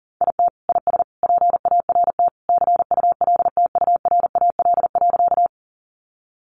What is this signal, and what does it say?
Morse code, heard as a keyed tone, about 40 wpm, 700 Hz. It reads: IT IS PART CULTURAL.
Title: Sentences from Top 1500 Words 40wpm